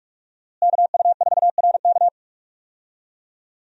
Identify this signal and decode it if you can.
Morse code, heard as a keyed tone, about 46 wpm, 700 Hz. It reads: KU4RK
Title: USA Call Signs - 5 Characters - Repeated 2x - 45wpm